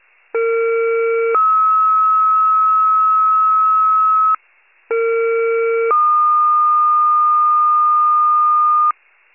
Two-tone page, heard as a one-second tone, then a three-second tone, about 460 then 1290 Hz, about 460 then 1220 Hz. It is heard from 0.3 to 8.9 s.
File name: wcalert